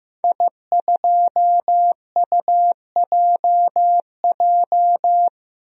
Morse code, heard as a keyed tone, about 15 wpm, 700 Hz. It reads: I2UJJ
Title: DXCC-100 Call Signs - 5 Characters - 15wpm